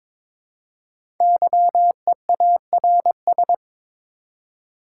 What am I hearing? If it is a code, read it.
Morse code, heard as a keyed tone, about 22 wpm, 700 Hz. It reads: YEARS
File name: Build A Phrase - 2-Word Phrases - 22wpm